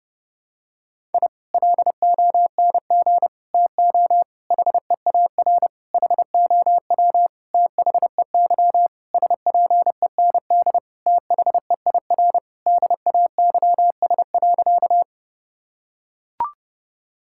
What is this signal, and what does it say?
Morse code, heard as a keyed tone, about 30 wpm, 700 Hz. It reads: I LONG TO HEAR HOW THEY SPEND THEIR DAYS.
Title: Sentences from Top 1000 Words - Review Part 5 - 30wpm